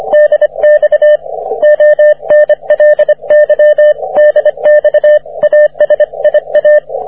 Morse code, heard as a keyed tone, about 25 wpm, 600 Hz, with radio noise. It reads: DX ONLY DX ASIA
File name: wp / ts7c 30m cw